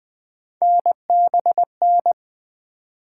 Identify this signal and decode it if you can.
Morse code, heard as a keyed tone, about 20 wpm, 700 Hz. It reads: NBN
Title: Three Letter-Number Combinations - 20wpm